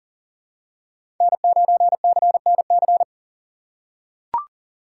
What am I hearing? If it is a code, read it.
Morse code, heard as a keyed tone, about 39 wpm, 700 Hz. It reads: N9CNC
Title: USA Call Signs - 5 Characters - 4x Character Spacing - 40wpm